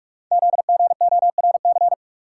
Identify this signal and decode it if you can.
Morse code, heard as a keyed tone, about 44 wpm, 690 Hz. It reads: ZGORC
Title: Code Groups - 5 Letters - 45wpm